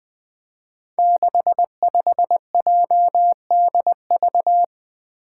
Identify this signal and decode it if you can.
Morse code, heard as a keyed tone, about 20 wpm, 700 Hz. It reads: 65JDV
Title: Five Challenging Letter-Number Combinations 20wpm